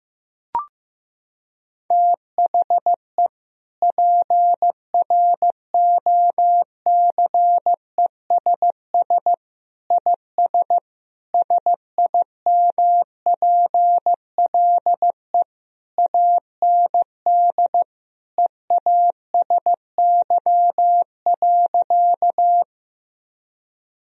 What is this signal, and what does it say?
Morse code, heard as a keyed tone, about 16 wpm, 700 Hz. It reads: THE PROCESS IS SIMPLE AND EASY.
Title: Sentences from Top 1000 Words - Review Part 3 - 15wpm